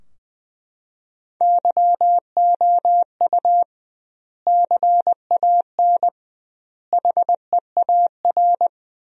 Morse code, heard as a keyed tone, about 21 wpm, 700 Hz. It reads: YOU CAN HEAR
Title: 2x Word-Spacing - Sentences from Top 500 Words - 20wpm